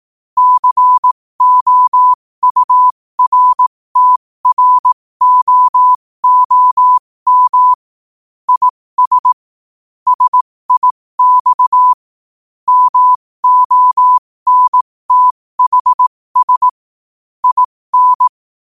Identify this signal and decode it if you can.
Morse code, heard as a keyed tone, about 18 wpm, spaced as for 15 wpm, 1000 Hz. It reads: COURTROOM IS SIX MONTHS IN
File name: qotdfiles / 15wpm